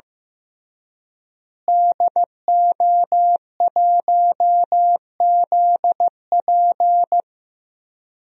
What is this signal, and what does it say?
Morse code, heard as a keyed tone, about 15 wpm, 700 Hz. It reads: DO1ZP
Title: DXCC-100 Call Signs - 5 Characters - Repeated 3x - 15wpm